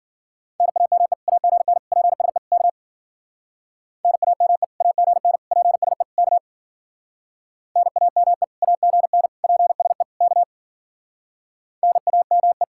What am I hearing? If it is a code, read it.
Morse code, heard as a keyed tone, about 48 wpm, 700 Hz. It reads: NAME AGN PSE K NAME AGN PSE K NAME AGN PSE K NAME
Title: Speed-Racing Common QSO Phrases - 40wpm